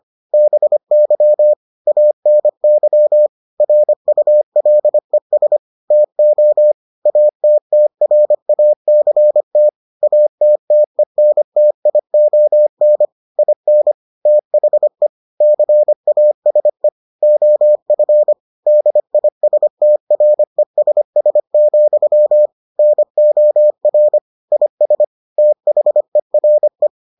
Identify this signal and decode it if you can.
Morse code, heard as a keyed tone, about 25 wpm, 600 Hz. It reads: BY ANY RULES TO ATTRACT ATTENTION IN THE CASE OF DISTRESS, NOR IS THERE